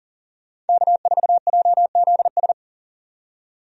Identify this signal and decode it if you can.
Morse code, heard as a keyed tone, about 40 wpm, 700 Hz. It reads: K4JZS
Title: USA Call Signs - 5 Characters - 40wpm